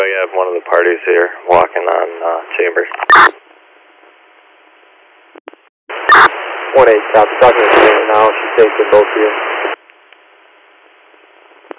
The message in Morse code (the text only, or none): none